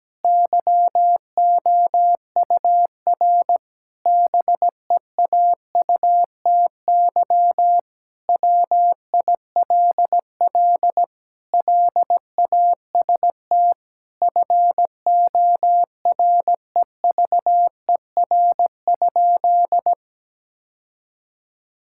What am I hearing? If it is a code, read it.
Morse code, heard as a keyed tone, about 17 wpm, 700 Hz. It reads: YOUR BEAUTY WILL LAST FOREVER?
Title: Sentences from Top 1600 Words 17wpm